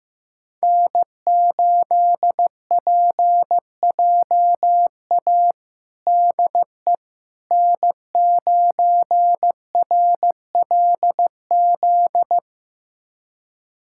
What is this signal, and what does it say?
Morse code, heard as a keyed tone, about 15 wpm, 700 Hz. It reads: N8PJA DE N9RLZ
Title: USA Callsign DE Callsign - 5 Characters - 15wpm